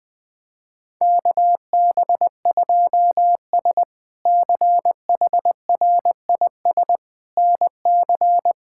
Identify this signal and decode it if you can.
Morse code, heard as a keyed tone, about 20 wpm, 700 Hz. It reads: KB2S CHRIS NC
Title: K1USN SST QSO Practice - 20wpm